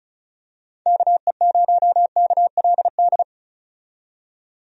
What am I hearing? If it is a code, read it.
Morse code, heard as a keyed tone, about 34 wpm, 700 Hz. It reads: KE0KLD
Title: USA Call Signs - 6 Characters - 35wpm